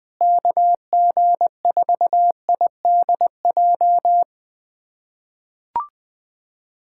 Morse code, heard as a keyed tone, about 20 wpm, 700 Hz. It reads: KG4IDJ
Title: USA Callsign DE Callsign 20wpm